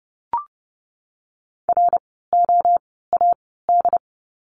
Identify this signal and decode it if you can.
Morse code, heard as a keyed tone, about 30 wpm, spaced as for 17 wpm, 700 Hz. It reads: ROAD
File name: Lesson 17f - Letter u - Words 30wpm